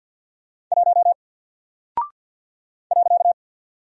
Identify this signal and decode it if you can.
Morse code, heard as a keyed tone, about 48 wpm, 700 Hz. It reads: 1 .
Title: Single Character (Letter:Number:Punctuation:Prosign) 50wpm